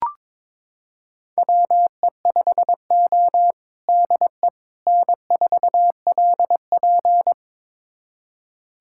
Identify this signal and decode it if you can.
Morse code, heard as a keyed tone, about 22 wpm, 700 Hz. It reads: WE5O DE N4LP
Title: USA Callsign DE Callsign - 4 Characters - 22wpm